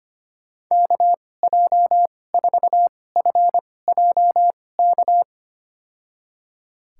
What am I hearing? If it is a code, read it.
Morse code, heard as a keyed tone, about 25 wpm, spaced as for 18 wpm, 700 Hz. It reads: KJ4FJK
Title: USA Call Signs - 6 Characters - 2x Character Spacing - 25wpm